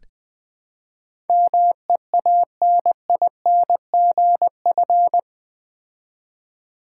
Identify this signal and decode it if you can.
Morse code, heard as a keyed tone, about 20 wpm, 700 Hz. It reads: MEANINGF
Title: Build A Word - 4 Letters At A Time - 10-Letter Words - 20wpm